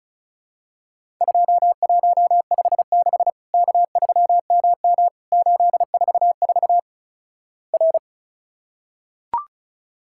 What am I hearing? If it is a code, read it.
Morse code, heard as a keyed tone, about 36 wpm, 700 Hz. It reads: 2156 K3MM 844 R
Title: Worked All Europe DX Contest - QTC Exchange - 35wpm